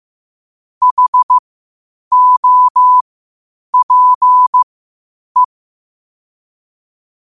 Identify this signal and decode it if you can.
Morse code, heard as a keyed tone, about 15 wpm, spaced as for 8 wpm, 1000 Hz. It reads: HOPE
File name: qotdfiles / 8wpm